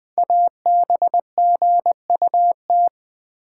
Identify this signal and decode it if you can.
Morse code, heard as a keyed tone, about 20 wpm, 700 Hz. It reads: ABGUT
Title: Code Groups - 5 Letters - 20wpm